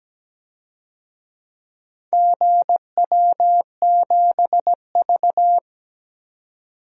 Morse code, heard as a keyed tone, about 17 wpm, 700 Hz. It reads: GW7V
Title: DXCC-100 Call Signs - 4 Characters - 17wpm